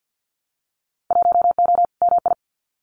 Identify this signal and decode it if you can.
Morse code, heard as a keyed tone, about 50 wpm, 700 Hz. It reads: 1O MI
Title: Winter Field Day Exchanges 50wpm